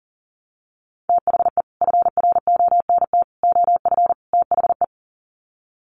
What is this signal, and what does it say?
Morse code, heard as a keyed tone, about 40 wpm, 700 Hz. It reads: THE FRONT OF THE
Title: Sets of 4 Words 40wpm